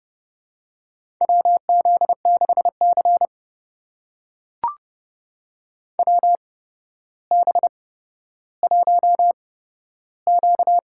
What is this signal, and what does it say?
Morse code, heard as a keyed tone, about 30 wpm, spaced as for 13 wpm, 700 Hz. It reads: WZ6C WB1Q
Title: USA Call Signs - 4 Characters - 8x Character Spacing - 30wpm